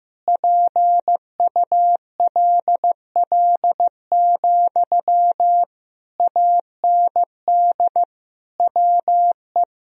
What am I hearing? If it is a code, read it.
Morse code, heard as a keyed tone, about 15 wpm, 700 Hz. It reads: PULL, AND WE
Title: Sentences from Top 1000 Words - Review Part 2 - 15wpm